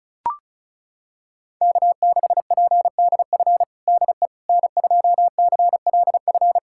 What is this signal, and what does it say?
Morse code, heard as a keyed tone, about 34 wpm, 690 Hz. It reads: K6PDF DE N2CLF
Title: USA Callsign DE Callsign - 5 Characters - 35wpm